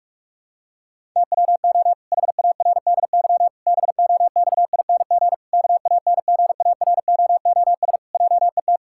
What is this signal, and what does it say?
Morse code, heard as a keyed tone, about 41 wpm, 700 Hz. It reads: TWO HARDY BOXING KANGAROOS JET
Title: Pangram Challenge v2 - 45wpm